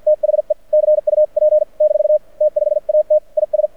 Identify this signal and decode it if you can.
Morse code, heard as a keyed tone, about 50 wpm, spaced as for 44 wpm, 600 Hz. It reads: THE CUP - T5AT IS
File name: archived / CRSnet